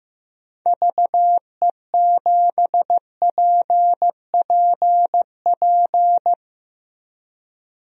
Morse code, heard as a keyed tone, about 15 wpm, 700 Hz. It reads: VE7PPP
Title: DXCC-100 Call Signs - 6 Characters - Repeated 3x - 15wpm